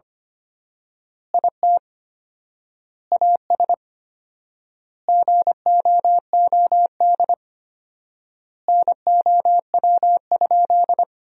Morse code, heard as a keyed tone, about 25 wpm, 700 Hz. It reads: IT AS GOOD NOW?
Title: 4x Word-Spacing - Sentences from Top 200 Words - 25wpm